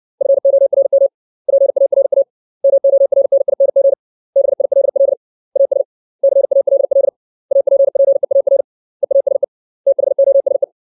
Morse code, heard as a keyed tone, about 62 wpm, 550 Hz. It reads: FORM PARA MORNING BILL AS CALL APPEAR EASE THOSE
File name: 632 words / 632 Words 60WPM